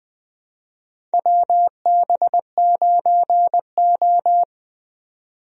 Morse code, heard as a keyed tone, about 20 wpm, 700 Hz. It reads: WB9O
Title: USA Call Signs - 4 Characters - 20wpm